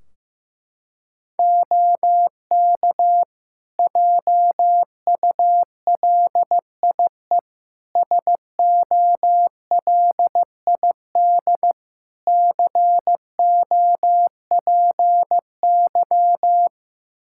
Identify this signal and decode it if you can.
Morse code, heard as a keyed tone, about 15 wpm, 700 Hz. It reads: OK JULIE SOLID COPY
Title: Practice QSO - Exchange 3 - More Info - 15wpm